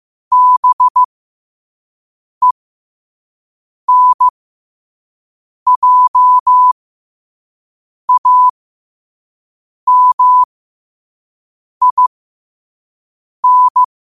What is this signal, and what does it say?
Morse code, heard as a keyed tone, about 15 wpm, spaced as for 5 wpm, 1000 Hz. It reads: BENJAMIN
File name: qotdfiles / 5wpm